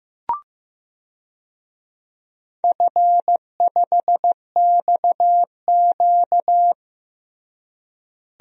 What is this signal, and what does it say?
Morse code, heard as a keyed tone, about 16 wpm, 700 Hz. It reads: F5XQ
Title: DXCC-100 Call Signs - 4 Characters - Repeated 3x - 15wpm